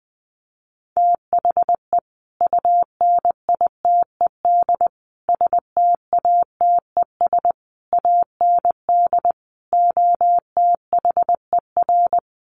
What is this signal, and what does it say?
Morse code, heard as a keyed tone, about 19 wpm, 700 Hz. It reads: THE UNITED STATES AND OTHER
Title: Sets of 5 Words Encore 20wpm